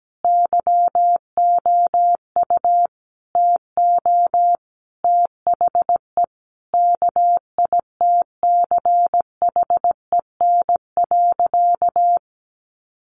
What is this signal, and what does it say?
Morse code, heard as a keyed tone, about 17 wpm, 700 Hz. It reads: YOU TO THE KITCHEN.
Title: Sentences from Top 1400 Words 17wpm